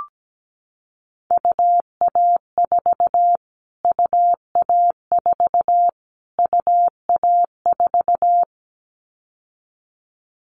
Morse code, heard as a keyed tone, about 17 wpm, 700 Hz. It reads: UA4 UA4 UA4
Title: Callsign Prefixes for Top 25 DXCC - Repeated 3x - 17wpm